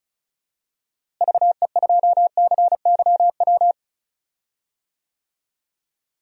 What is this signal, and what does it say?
Morse code, heard as a keyed tone, about 36 wpm, 700 Hz. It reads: VE2CYW
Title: DXCC-100 Call Signs - 6 Characters - Repeated 3x - 35wpm